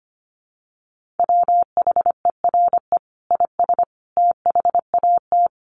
Morse code, heard as a keyed tone, about 25 wpm, 700 Hz. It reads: WHERE IS THAT
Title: Sentences from Top 1300 Words 25wpm